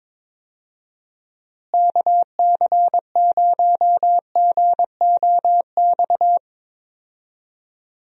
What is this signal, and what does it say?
Morse code, heard as a keyed tone, about 22 wpm, 700 Hz. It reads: KC0GOX
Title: USA Call Signs - 6 Characters - Repeated 3x - 22wpm